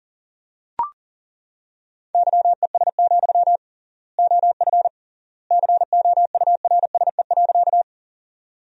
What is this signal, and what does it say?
Morse code, heard as a keyed tone, about 40 wpm, 700 Hz. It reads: YES, OF COURSE.
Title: 3x Word-Spacing - Sentences from Top 500 Words - 40wpm